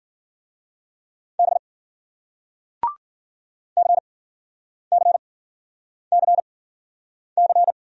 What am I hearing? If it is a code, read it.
Morse code, heard as a keyed tone, about 53 wpm, spaced as for 9 wpm, 700 Hz. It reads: B CCCC